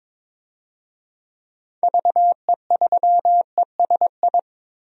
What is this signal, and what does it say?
Morse code, heard as a keyed tone, about 22 wpm, 700 Hz. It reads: VE3ESI
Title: DXCC-100 Call Signs - 6 Characters - Repeated 2x - 22wpm